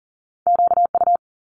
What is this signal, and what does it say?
Morse code, heard as a keyed tone, about 39 wpm, 700 Hz. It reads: QU
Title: Top Two Letter Combinations in Words 40wpm